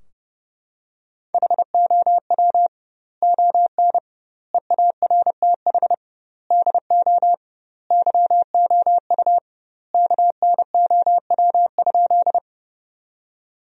Morse code, heard as a keyed tone, about 30 wpm, 700 Hz. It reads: HOW ON EARTH DO YOU KNOW?
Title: 2x Word-Spacing - Sentences from Top 200 Words - 30wpm